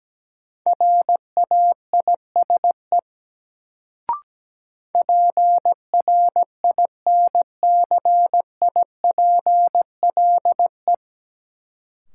Morse code, heard as a keyed tone, about 17 wpm, 700 Hz. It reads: RAISE PRINCIPLE
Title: Sentences from Top 1400 Words - New Word Drill - 17wpm